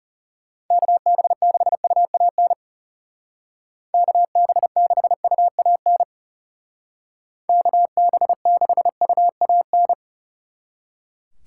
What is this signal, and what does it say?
Morse code, heard as a keyed tone, about 37 wpm, 700 Hz. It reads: KB6UAN KB6UAN KB6UAN
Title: Speed Racing - USA 2x3 Call Signs 30wpm